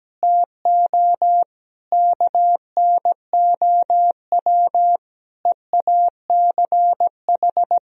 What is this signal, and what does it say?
Morse code, heard as a keyed tone, about 17 wpm, 700 Hz. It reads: TO KNOW EACH
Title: Sentences from Top 1000 Words - Review Part 1 - 17wpm